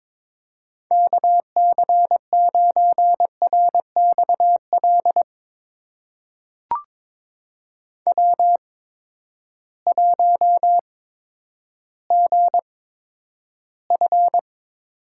Morse code, heard as a keyed tone, about 22 wpm, spaced as for 11 wpm, 700 Hz. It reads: KC9RXL W1GF
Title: USA Call Signs - 8x Character Spacing - 22wpm